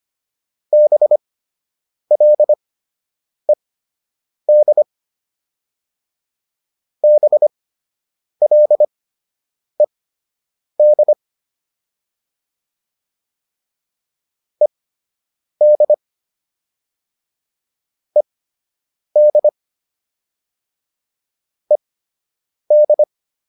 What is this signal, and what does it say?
Morse code, heard as a keyed tone, about 25 wpm, spaced as for 8 wpm, 600 Hz. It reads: BLED BLED ED ED ED
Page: fundamental-practice-files